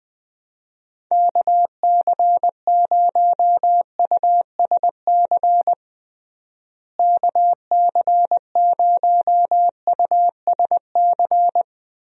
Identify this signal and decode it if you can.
Morse code, heard as a keyed tone, about 21 wpm, 700 Hz. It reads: KC0USC KC0USC
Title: USA Call Signs - 6 Characters - Repeated 3x - 20wpm